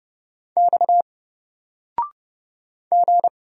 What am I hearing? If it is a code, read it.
Morse code, heard as a keyed tone, about 31 wpm, 700 Hz. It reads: X G
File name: Single-Letter-Number-30wpm